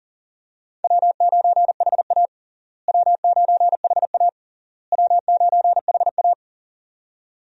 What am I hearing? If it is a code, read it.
Morse code, heard as a keyed tone, about 40 wpm, 700 Hz. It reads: W9HA W9HA W9HA
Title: USA Call Signs - 4 Characters - Repeated 3x - 40wpm